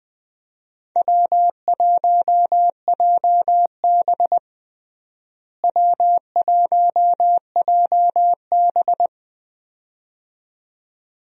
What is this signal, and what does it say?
Morse code, heard as a keyed tone, about 20 wpm, 700 Hz. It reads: W1JB W1JB
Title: USA Call Signs - 4 Characters - Repeated 2x - 20wpm